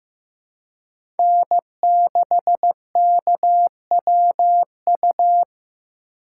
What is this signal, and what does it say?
Morse code, heard as a keyed tone, about 15 wpm, 700 Hz. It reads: N6KWU
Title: USA Call Signs - 5 Characters - 2x Character Spacing - 15wpm